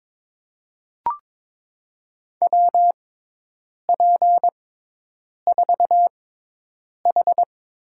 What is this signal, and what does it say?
Morse code, heard as a keyed tone, about 21 wpm, spaced as for 8 wpm, 700 Hz. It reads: WP4H
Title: USA Call Signs - 5 Characters - 6x Character Spacing - 22wpm